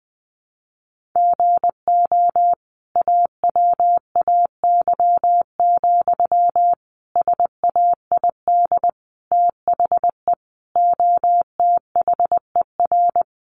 Morse code, heard as a keyed tone, about 20 wpm, 700 Hz. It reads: GO AWAY, SAID THE OTHER
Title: Sentences from Top 1500 Words 20wpm